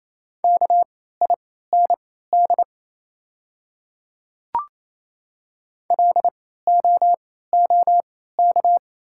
Morse code, heard as a keyed tone, about 28 wpm, spaced as for 16 wpm, 700 Hz. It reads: KIND LOOK
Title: 3x Character Spacing - Top 200 Words - 28wpm